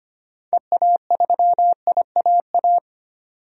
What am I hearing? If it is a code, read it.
Morse code, heard as a keyed tone, about 25 wpm, 690 Hz. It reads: EA3IAA
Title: DXCC-100 Call Signs - 6 Characters - 25wpm